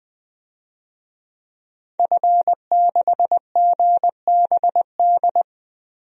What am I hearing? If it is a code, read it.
Morse code, heard as a keyed tone, about 20 wpm, 700 Hz. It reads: F6GBD
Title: DXCC-100 Call Signs - 5 Characters - 20wpm